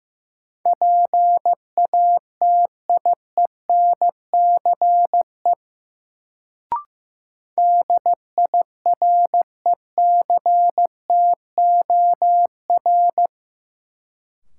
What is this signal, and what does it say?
Morse code, heard as a keyed tone, about 16 wpm, 700 Hz. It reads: PATIENCE DIRECTOR
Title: Sentences from Top 1300 Words - New Word Drill Alt - 15wpm